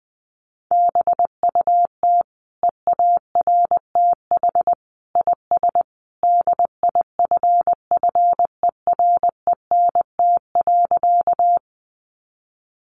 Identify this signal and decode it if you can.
Morse code, heard as a keyed tone, about 20 wpm, 700 Hz. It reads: BUT EARTH IS DIFFERENT.
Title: Sentences from Top 600 Words 20wpm